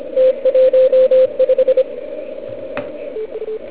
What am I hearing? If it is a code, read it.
Morse code, heard as a keyed tone, about 26 wpm, 520 Hz. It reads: T15